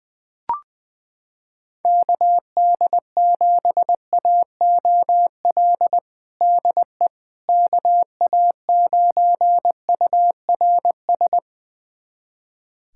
Morse code, heard as a keyed tone, about 20 wpm, 700 Hz. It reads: KD7AOL DE KA9URS
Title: USA Callsign DE Callsign 20wpm